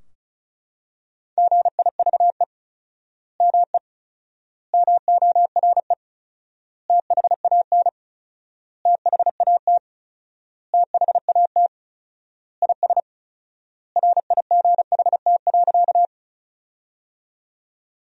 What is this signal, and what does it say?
Morse code, heard as a keyed tone, about 36 wpm, 700 Hz. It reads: GIVE ME MORE THAN THAT THAT IS RIGHT.
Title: 4x Word-Spacing - Sentences from Top 200 Words - 35wpm